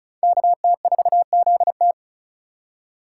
Morse code, heard as a keyed tone, about 34 wpm, 700 Hz. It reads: KT4ZT
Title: USA Call Signs - 5 Characters - 2x Character Spacing - 35wpm